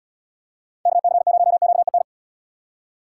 Morse code, heard as a keyed tone, about 70 wpm, 700 Hz. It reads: KC0QEN